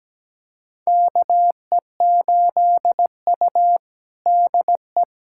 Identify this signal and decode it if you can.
Morse code, heard as a keyed tone, about 17 wpm, 700 Hz. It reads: KE8U DE
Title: USA Callsign DE Callsign - 4 Characters - 17wpm